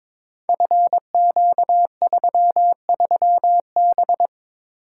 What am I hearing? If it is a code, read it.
Morse code, heard as a keyed tone, about 22 wpm, 700 Hz. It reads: FQ33B
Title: Code Groups - 5 Letter:Numbers - 22wpm